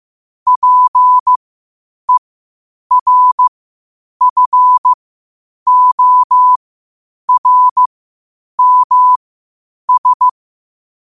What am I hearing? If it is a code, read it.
Morse code, heard as a keyed tone, about 15 wpm, spaced as for 8 wpm, 1000 Hz. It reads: PERFORMS
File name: qotdfiles / 8wpm